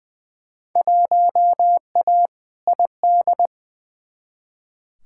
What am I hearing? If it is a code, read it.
Morse code, heard as a keyed tone, about 20 wpm, 700 Hz. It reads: 1A ID
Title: Field Day Exchanges - 20wpm